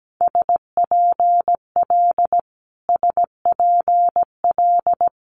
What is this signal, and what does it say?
Morse code, heard as a keyed tone, about 17 wpm, 700 Hz. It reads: SPL SPL
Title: Multi-letter Phonemes Repeated 3x 17wpm